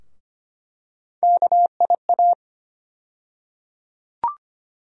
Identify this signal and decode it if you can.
Morse code, heard as a keyed tone, about 25 wpm, 700 Hz. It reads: KIA
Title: USA Call Sign Postfixes - 3 Letters 25wpm